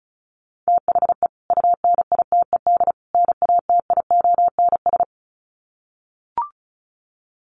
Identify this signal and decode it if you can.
Morse code, heard as a keyed tone, about 34 wpm, 700 Hz. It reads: THE UNITED NATIONS
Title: Sets of 3 Words 35wpm